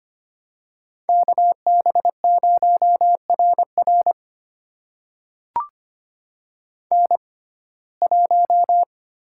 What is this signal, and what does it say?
Morse code, heard as a keyed tone, about 25 wpm, spaced as for 18 wpm, 700 Hz. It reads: KB0RR N1
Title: USA Call Signs - 5 Characters - 6x Character Spacing - 25wpm